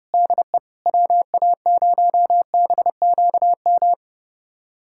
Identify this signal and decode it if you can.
Morse code, heard as a keyed tone, about 30 wpm, 700 Hz. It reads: DE WA0BQM
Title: USA Callsign DE Callsign - 6 Characters - 30wpm